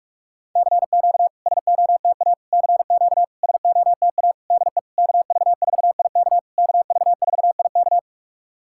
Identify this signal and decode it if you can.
Morse code, heard as a keyed tone, about 46 wpm, 700 Hz. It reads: CQ SOTA CQ SOTA DE KV4IK KV4IK
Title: SOTA QSO Practice 45wpm